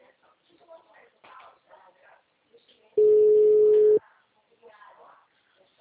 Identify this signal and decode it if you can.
Ringback tone (Europe, 425 Hz), with speech.
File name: out- -2029- - - .1470